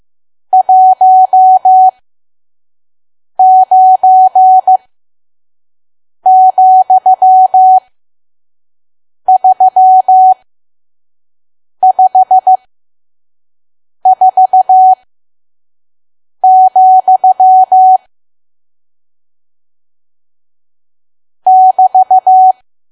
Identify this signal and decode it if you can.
Morse code, heard as a keyed tone, about 15 wpm, spaced as for 5 wpm, 750 Hz. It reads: 19,354, <BT>
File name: Archive / 5 WPM files / PM